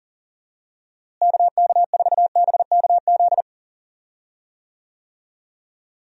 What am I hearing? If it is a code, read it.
Morse code, heard as a keyed tone, about 40 wpm, 700 Hz. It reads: KK4BKZ
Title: USA Call Signs - 6 Characters - Repeated 3x - 40wpm